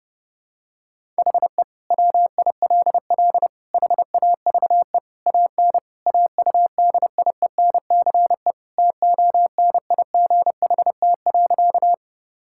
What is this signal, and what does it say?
Morse code, heard as a keyed tone, about 31 wpm, 700 Hz. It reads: HE WILL HAVE AN AUDIENCE TONIGHT.
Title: Sentences from Top 1000 Words - Review Part 5 - 30wpm